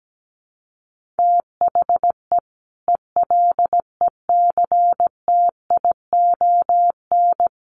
Morse code, heard as a keyed tone, about 17 wpm, 700 Hz. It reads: THE ELECTION
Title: Sentences from Top 1500 Words 17wpm